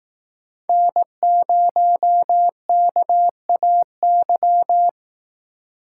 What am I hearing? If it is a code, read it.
Morse code, heard as a keyed tone, about 18 wpm, 700 Hz. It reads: N0KAY
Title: Speed Racing - USA 1x3 Call Signs 15wpm